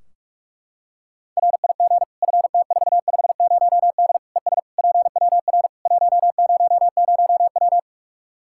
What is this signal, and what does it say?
Morse code, heard as a keyed tone, about 46 wpm, 700 Hz. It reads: RIG FT450D ES PWR 100W
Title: K1USN SST QSO Practice 45wpm